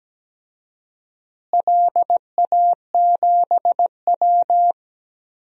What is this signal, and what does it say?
Morse code, heard as a keyed tone, about 17 wpm, 700 Hz. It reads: LA7W
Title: DXCC-100 Call Signs - 4 Characters - Repeated 3x - 17wpm